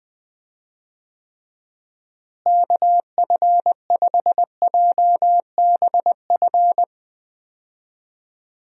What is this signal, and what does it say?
Morse code, heard as a keyed tone, about 20 wpm, 700 Hz. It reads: KF5JBF
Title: USA Call Signs - 6 Characters - Repeated 2x - 20wpm